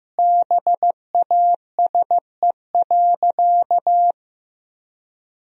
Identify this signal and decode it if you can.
Morse code, heard as a keyed tone, about 15 wpm, 700 Hz. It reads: BASE.
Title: Sentences from Top 1000 Words - Review Part 1 - 15wpm